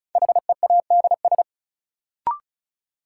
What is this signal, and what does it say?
Morse code, heard as a keyed tone, about 36 wpm, 700 Hz. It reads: HEADS
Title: Sentences from Top 1000 Words - Review Part 3 - New Word Drill - 35wpm